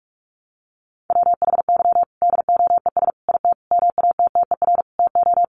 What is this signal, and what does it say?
Morse code, heard as a keyed tone, about 44 wpm, 700 Hz. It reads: WHY DOES IT MATTER TO
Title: Sentences from Top 700 Words 45wpm